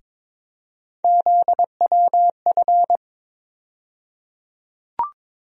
Morse code, heard as a keyed tone, about 22 wpm, 700 Hz. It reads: ZWF
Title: USA Call Sign Postfixes - 3 Letters 22wpm